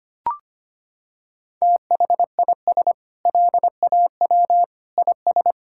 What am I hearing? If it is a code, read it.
Morse code, heard as a keyed tone, about 25 wpm, 700 Hz. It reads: THIS LAW IS
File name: practice / 25wpm / Sentences with 4-Word Phrases - 25wpm